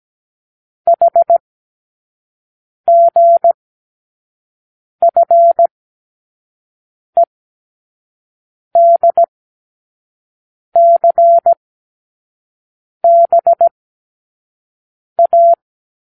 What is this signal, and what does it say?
Morse code, heard as a keyed tone, about 17 wpm, spaced as for 5 wpm, 680 Hz. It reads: HGFEDCBA
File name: k7qo / Lesson12(A-H)